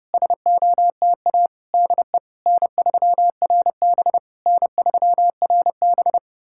Morse code, heard as a keyed tone, about 30 wpm, 700 Hz. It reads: SOTA DE N3RB N3RB
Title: SOTA QSO Practice 30wpm